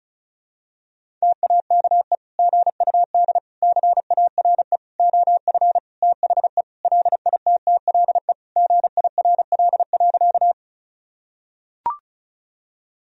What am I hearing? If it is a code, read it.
Morse code, heard as a keyed tone, about 34 wpm, 700 Hz. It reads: TAKE GUD CARE OF THE LITTLE GIRL.
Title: Sentences from Top 400 Words and Abbreviations 35wpm